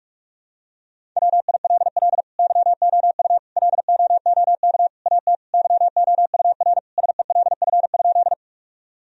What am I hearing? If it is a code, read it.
Morse code, heard as a keyed tone, about 42 wpm, 700 Hz. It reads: WILL YOU LOOK AT YOUR SELF?
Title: Sentences from Top 1000 Words - Review Part 1 - 45wpm